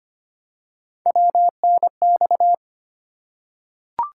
Morse code, heard as a keyed tone, about 25 wpm, 700 Hz. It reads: WNX
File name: Three Letter-Number Combinations - 25wpm